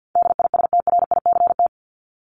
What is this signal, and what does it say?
Morse code, heard as a keyed tone, about 50 wpm, 700 Hz. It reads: DISTRICT
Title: Top 1501-1750 Words 50wpm